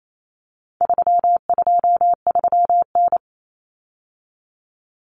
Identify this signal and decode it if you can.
Morse code, heard as a keyed tone, about 28 wpm, 700 Hz. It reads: 323N
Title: Common Cut Numbers - 4-Digit 28wpm